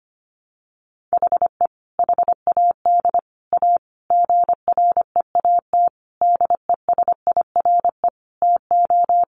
Morse code, heard as a keyed tone, about 25 wpm, 700 Hz. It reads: HE HAD A GREAT DESIRE TO
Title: Sentences from Top 1100 Words 25wpm